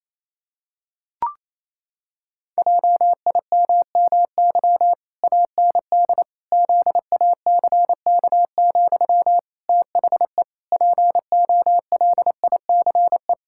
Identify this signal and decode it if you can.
Morse code, heard as a keyed tone, about 27 wpm, 700 Hz. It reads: JIMMY AND ZACK, THE POLICE
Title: Pangram Challenge v2 - 28wpm